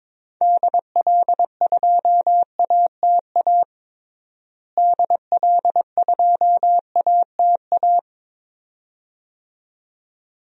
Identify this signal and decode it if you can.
Morse code, heard as a keyed tone, about 22 wpm, 700 Hz. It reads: DL2ATA DL2ATA
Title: DXCC-100 Call Signs - 6 Characters - Repeated 2x - 22wpm